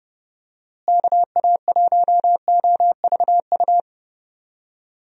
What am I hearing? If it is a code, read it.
Morse code, heard as a keyed tone, about 31 wpm, 700 Hz. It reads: KA1OVU
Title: USA Call Signs - 6 Characters - Repeated 3x - 30wpm